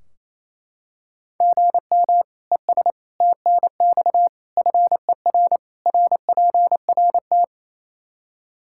Morse code, heard as a keyed tone, about 28 wpm, 700 Hz. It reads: GM ES TNX FER RPRT
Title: Practice QSO - Exchange 1 - Beginning - 28wpm